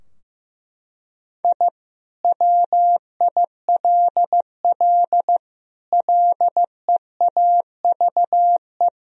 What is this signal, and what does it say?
Morse code, heard as a keyed tone, about 15 wpm, 700 Hz. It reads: I WILL LEAVE
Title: Sentences from Top 1000 Words - Review Part 4 - 15wpm